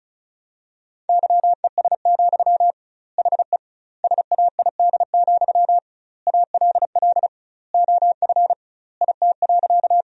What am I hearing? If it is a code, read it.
Morse code, heard as a keyed tone, about 36 wpm, 700 Hz. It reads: YES, HE SAID, ALL OF IT.
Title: 2x Word-Spacing - Sentences from Top 500 Words - 35wpm